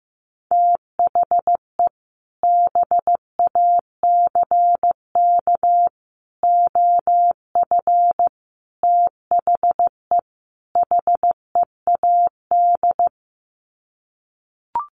Morse code, heard as a keyed tone, about 15 wpm, 700 Hz. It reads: THE BACK OF THE HEAD
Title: Sets of 5 Words Encore2 15wpm